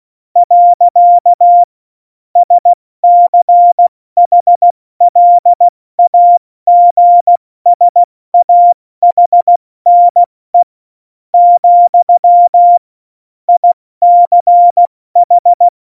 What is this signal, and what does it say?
Morse code, heard as a keyed tone, about 16 wpm, spaced as for 13 wpm, 700 Hz. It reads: . SCHLAGSAHNE, ICH
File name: Morsekurs / Witze / MP3, Witz 24 April 2018, 70 F80